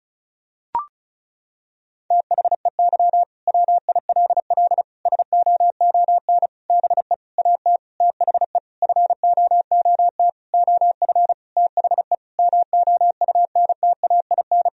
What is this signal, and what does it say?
Morse code, heard as a keyed tone, about 34 wpm, 700 Hz. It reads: THEY WILL SOON BE AT THE FOOT OF THE MOUNTAIN
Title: Sentences from Top 1000 Words - Review Part 4 - 35wpm